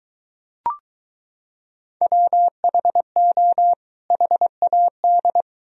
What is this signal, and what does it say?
Morse code, heard as a keyed tone, about 22 wpm, 700 Hz. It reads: WHO HAD